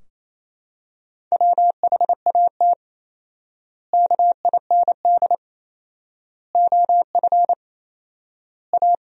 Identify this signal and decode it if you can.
Morse code, heard as a keyed tone, about 28 wpm, 700 Hz. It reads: WHAT KIND OF A
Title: 4x Word-Spacing - Sentences from Top 400 Words - 28wpm